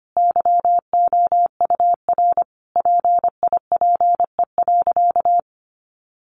Morse code, heard as a keyed tone, about 25 wpm, 700 Hz. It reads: YOUR PIPE.
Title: Sentences from Top 1500 Words 25wpm